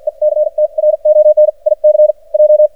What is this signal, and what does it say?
Morse code, heard as a keyed tone, about 53 wpm, spaced as for 42 wpm, 610 Hz. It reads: ECTRONIC J
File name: archived / CRSnet